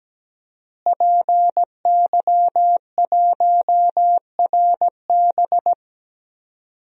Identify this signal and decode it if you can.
Morse code, heard as a keyed tone, about 17 wpm, 700 Hz. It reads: PY1RB
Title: DXCC-100 Call Signs - 5 Characters - Repeated 2x - 17wpm